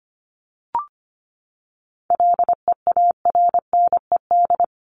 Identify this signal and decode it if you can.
Morse code, heard as a keyed tone, about 25 wpm, 700 Hz. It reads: LEARNED
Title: ED Suffix - Top 100 Words with Base Words 25wpm